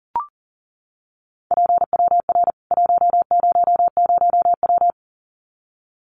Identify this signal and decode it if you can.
Morse code, heard as a keyed tone, about 40 wpm, 700 Hz. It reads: PWR 100W
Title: Common QSO Phrases 40wpm